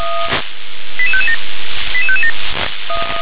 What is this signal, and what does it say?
Five-tone selective call (ZVEI), heard as quick runs of five tones, tones of about 70 ms. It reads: 88557 88557